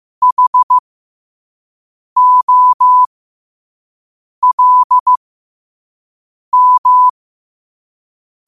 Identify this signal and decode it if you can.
Morse code, heard as a keyed tone, about 15 wpm, spaced as for 5 wpm, 1000 Hz. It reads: HOLM
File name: qotdfiles / 5wpm